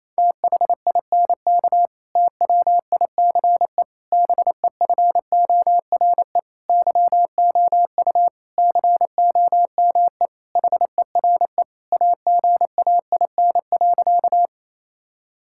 Morse code, heard as a keyed tone, about 28 wpm, 700 Hz. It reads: THINK TWICE BEFORE YOU COME HERE AGAIN.
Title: Sentences from Top 1600 Words 28wpm